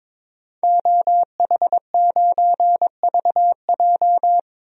Morse code, heard as a keyed tone, about 22 wpm, 700 Hz. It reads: OH9VJ
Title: DXCC-100 Call Signs - 5 Characters - 22wpm